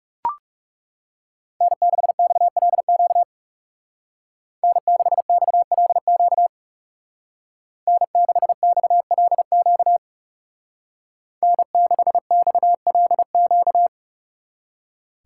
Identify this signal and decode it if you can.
Morse code, heard as a keyed tone, about 35 wpm, 700 Hz. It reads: N6XLQ N6XLQ N6XLQ N6XLQ
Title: Speed Racing - USA 1x3 Call Signs 30wpm